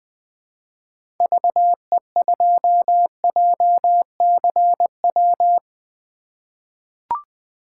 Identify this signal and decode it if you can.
Morse code, heard as a keyed tone, about 21 wpm, 700 Hz. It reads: VE2JCW
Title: 2023 SOTA Chaser Call Signs - 20wpm